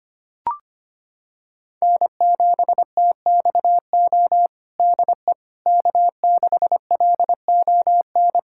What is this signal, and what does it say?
Morse code, heard as a keyed tone, about 25 wpm, 700 Hz. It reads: N7TXO DE K6LON
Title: USA Callsign DE Callsign - 5 Characters - 25wpm